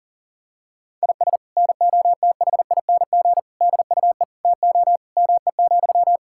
Morse code, heard as a keyed tone, about 37 wpm, 700 Hz. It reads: IS NOTHING DUE TO ME,
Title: Sentences from Top 1000 Words - Review Part 6 - 40wpm